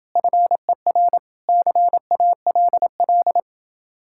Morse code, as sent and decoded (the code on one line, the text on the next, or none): ..-. .-. / -.-. .- .-.. .-..
FER CALL